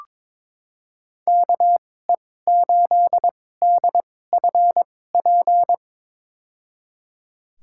Morse code, heard as a keyed tone, about 22 wpm, spaced as for 16 wpm, 700 Hz. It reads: KE8DFP
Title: USA Call Signs - 6 Characters - 2x Character Spacing - 22wpm